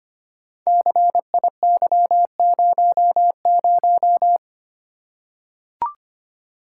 Morse code, heard as a keyed tone, about 25 wpm, 700 Hz. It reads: CIY00
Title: Code Groups - 5 Letter:Numbers - 25wpm